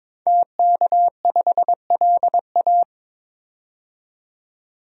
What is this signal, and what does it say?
Morse code, heard as a keyed tone, about 21 wpm, 700 Hz. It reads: TK5LA
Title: DXCC-100 Call Signs - 5 Characters - 22wpm